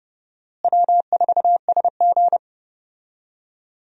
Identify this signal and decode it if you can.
Morse code, heard as a keyed tone, about 30 wpm, 700 Hz. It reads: W4SG
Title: USA Call Signs - 4 Characters - Repeated 2x - 30wpm